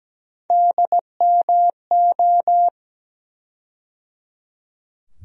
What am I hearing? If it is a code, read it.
Morse code, heard as a keyed tone, about 17 wpm, 700 Hz. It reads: DMO
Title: USA Call Sign Postfixes - 3 Letters 17wpm